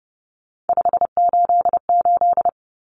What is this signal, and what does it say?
Morse code, heard as a keyed tone, about 30 wpm, 700 Hz. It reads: 588
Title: Numbers - 3-Digit 30wpm